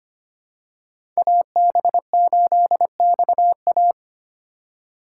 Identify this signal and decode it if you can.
Morse code, heard as a keyed tone, about 25 wpm, 700 Hz. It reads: AB8XA
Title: USA Call Signs - 5 Characters - 4x Character Spacing - 25wpm